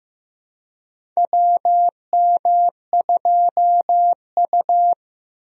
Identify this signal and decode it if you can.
Morse code, heard as a keyed tone, about 16 wpm, 700 Hz. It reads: WM2U
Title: USA Call Signs - 4 Characters - 15wpm